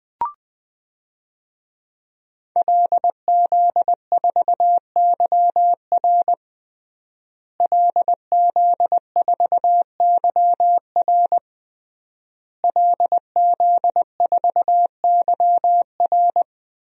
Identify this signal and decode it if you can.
Morse code, heard as a keyed tone, about 21 wpm, 700 Hz. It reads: LZ4YR LZ4YR LZ4YR
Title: DXCC-100 Call Signs - 5 Characters - Repeated 3x - 20wpm